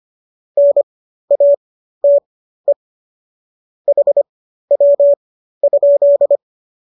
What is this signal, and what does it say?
Morse code, heard as a keyed tone, about 25 wpm, spaced as for 13 wpm, 570 Hz. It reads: NATE HW?
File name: qso211 13